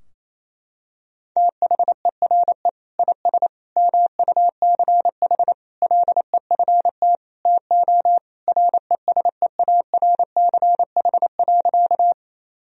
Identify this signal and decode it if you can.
Morse code, heard as a keyed tone, about 28 wpm, 700 Hz. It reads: THERE IS MUCH LEFT TO RESEARCH.
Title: Sentences from Top 1000 Words - Review Part 4 - 28wpm